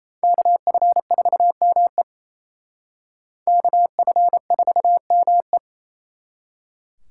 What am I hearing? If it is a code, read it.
Morse code, heard as a keyed tone, about 36 wpm, 700 Hz. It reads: KF4ME KF4ME